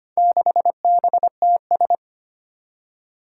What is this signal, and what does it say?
Morse code, heard as a keyed tone, about 25 wpm, 700 Hz. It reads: 6BTS
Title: Four Challenging Letter-Number Combinations 25wpm